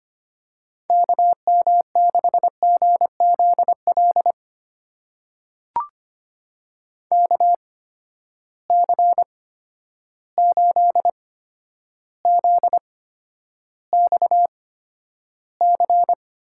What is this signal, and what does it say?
Morse code, heard as a keyed tone, about 25 wpm, spaced as for 11 wpm, 700 Hz. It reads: KM6GZL KC8ZXC
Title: USA Call Signs - 6 Characters - 8x Character Spacing - 25wpm